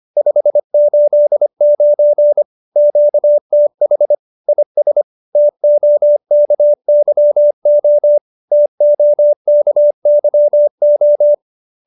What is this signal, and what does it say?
Morse code, heard as a keyed tone, about 25 wpm, 590 Hz. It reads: 589 QTH IS TOKYO TOKYO
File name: QSO305 25